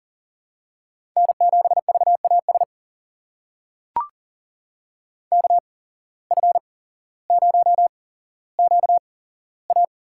Morse code, heard as a keyed tone, about 40 wpm, spaced as for 17 wpm, 700 Hz. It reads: N7VAS KF0QA
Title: USA Call Signs - 5 Characters - 8x Character Spacing - 40wpm